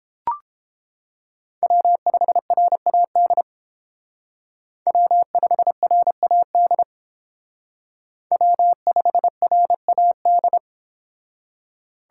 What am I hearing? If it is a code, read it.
Morse code, heard as a keyed tone, about 31 wpm, 700 Hz. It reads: W5RAD W5RAD W5RAD